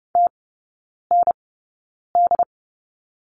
Morse code, heard as a keyed tone, about 30 wpm, spaced as for 9 wpm, 700 Hz. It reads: TND
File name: 081 Lesson 14e - Number 2 - Three Characters 30wpm